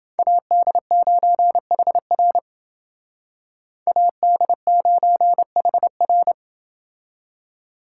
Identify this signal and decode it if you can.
Morse code, heard as a keyed tone, about 29 wpm, 700 Hz. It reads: AD9HR AD9HR